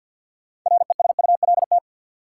Morse code, heard as a keyed tone, about 47 wpm, 700 Hz. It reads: RESULT
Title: Sentences from Top 700 Words - New Word Drill - 50wpm